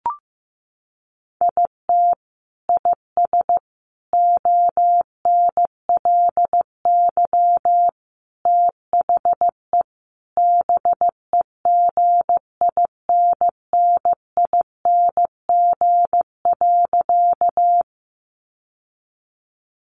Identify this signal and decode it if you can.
Morse code, heard as a keyed tone, about 15 wpm, 700 Hz. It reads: IT IS ONLY THE BEGINNING.
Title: Sentences from Top 800 Words 15wpm